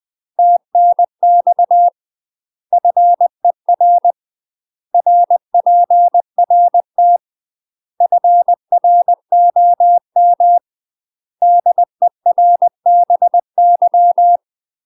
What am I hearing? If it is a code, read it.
Morse code, heard as a keyed tone, about 20 wpm, 700 Hz. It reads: TNX FER RPRT FROM DERBY
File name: UK MASTER QSOs TRAINING / 20WPM / UK MASTER QSOs 0049